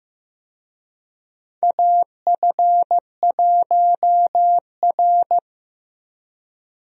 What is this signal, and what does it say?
Morse code, heard as a keyed tone, about 15 wpm, 700 Hz. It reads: AF1R
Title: USA Call Signs - 4 Characters - 15wpm